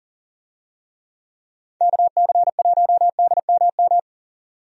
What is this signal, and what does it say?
Morse code, heard as a keyed tone, about 40 wpm, 700 Hz. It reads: KC1DMM
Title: USA Call Signs - 6 Characters - 40wpm